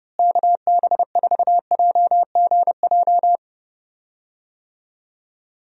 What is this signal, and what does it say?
Morse code, heard as a keyed tone, about 28 wpm, 700 Hz. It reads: KB4JGJ
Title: USA Call Signs - 6 Characters - Repeated 3x - 30wpm